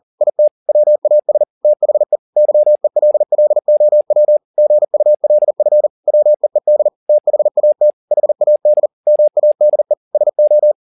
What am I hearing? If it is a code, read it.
Morse code, heard as a keyed tone, about 40 wpm, 600 Hz. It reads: IT WAS THE YELLOW GULF WEED THAT HAD MADE SO